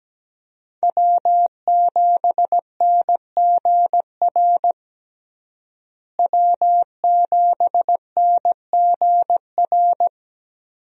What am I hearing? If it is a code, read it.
Morse code, heard as a keyed tone, about 17 wpm, 700 Hz. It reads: W7NGR W7NGR
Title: USA Call Signs - 5 Characters - Repeated 2x - 17wpm